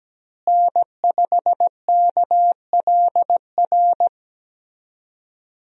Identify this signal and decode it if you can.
Morse code, heard as a keyed tone, about 17 wpm, 700 Hz. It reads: N5KLR
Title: USA Call Signs - 5 Characters - Repeated 3x - 17wpm